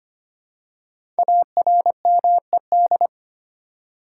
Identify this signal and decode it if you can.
Morse code, heard as a keyed tone, about 25 wpm, 700 Hz. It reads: ARMED
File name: 5 Letter Words - First 3 Letters Given - 25wpm